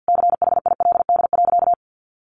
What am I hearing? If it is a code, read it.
Morse code, heard as a keyed tone, about 50 wpm, 700 Hz. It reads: CHILD.
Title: Sentences from Top 1500 Words 50wpm